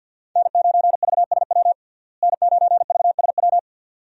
Morse code, heard as a keyed tone, about 52 wpm, 700 Hz. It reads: N9VSW N9VSW
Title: USA Call Signs - 5 Characters - Repeated 2x - 50wpm